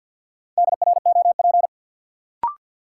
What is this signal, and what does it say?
Morse code, heard as a keyed tone, about 53 wpm, 700 Hz. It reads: DROP
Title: Sentences from Top 1400 Words - New Word Drill - 50wpm